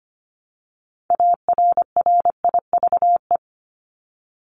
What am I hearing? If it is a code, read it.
Morse code, heard as a keyed tone, about 25 wpm, 700 Hz. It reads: ARRIVE
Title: ED Suffix - Top 100 Words with Base Words 25wpm